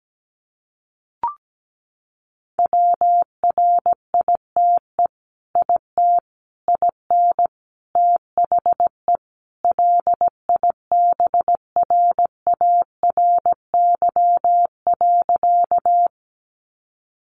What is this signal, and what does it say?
Morse code, heard as a keyed tone, about 17 wpm, 700 Hz. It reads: WRITE IT IN THE LIBRARY.
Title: Sentences from Top 600 Words 17wpm